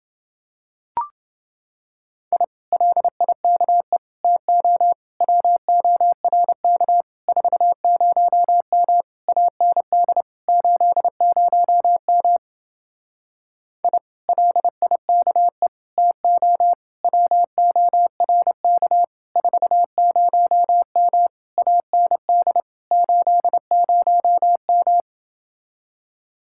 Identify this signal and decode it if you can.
Morse code, heard as a keyed tone, about 28 wpm, 700 Hz. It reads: I LIKE TO WORK 40M AND 80M I LIKE TO WORK 40M AND 80M